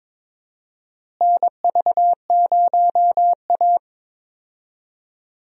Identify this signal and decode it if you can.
Morse code, heard as a keyed tone, about 22 wpm, 700 Hz. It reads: NV0A
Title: USA Call Signs - 4 Characters - Repeated 2x - 22wpm